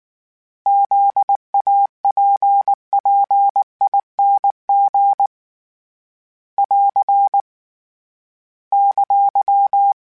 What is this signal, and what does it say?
Morse code, heard as a keyed tone, about 19 wpm, 800 Hz. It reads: ZAPPING <AR> !
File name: Special INGs / Special INGs-mp3s / Sngl Ing @18